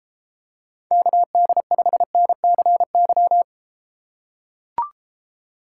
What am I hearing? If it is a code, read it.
Morse code, heard as a keyed tone, about 32 wpm, 700 Hz. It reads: KD5NCY